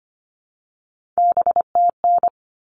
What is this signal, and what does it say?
Morse code, heard as a keyed tone, about 25 wpm, 700 Hz. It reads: BTN
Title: All Cut Numbers - 3-Digit 25wpm